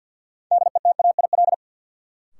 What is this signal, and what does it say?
Morse code, heard as a keyed tone, about 49 wpm, 700 Hz. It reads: DETAIL
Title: Sentences from Top 1500 Words - New Word Drill Alt - 50wpm